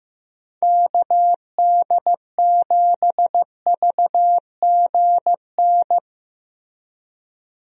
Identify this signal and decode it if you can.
Morse code, heard as a keyed tone, about 15 wpm, 700 Hz. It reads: KD7VGN
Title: USA Call Signs - 6 Characters - Repeated 2x - 15wpm